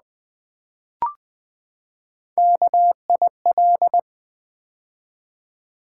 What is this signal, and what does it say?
Morse code, heard as a keyed tone, about 20 wpm, 700 Hz. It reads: KIL
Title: Build A Word - 3 Letters At A Time - 9-Letter Words - 20wpm